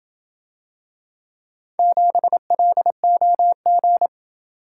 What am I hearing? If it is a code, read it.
Morse code, heard as a keyed tone, about 27 wpm, 700 Hz. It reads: 7LOG